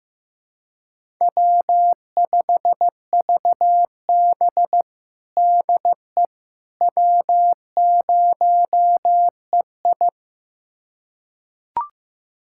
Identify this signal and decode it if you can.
Morse code, heard as a keyed tone, about 15 wpm, 700 Hz. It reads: W5VB DE W0EI
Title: USA Callsign DE Callsign - 4 Characters - 15wpm